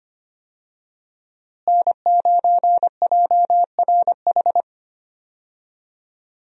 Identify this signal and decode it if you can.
Morse code, heard as a keyed tone, about 25 wpm, 700 Hz. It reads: N9JRH
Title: USA Call Signs - 5 Characters - Repeated 2x - 25wpm